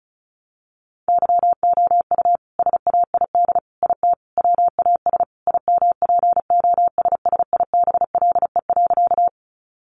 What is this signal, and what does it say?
Morse code, heard as a keyed tone, about 34 wpm, 700 Hz. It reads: YOU SAID IT WAS IMPOSSIBLE.
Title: Sentences from Top 1100 Words 35wpm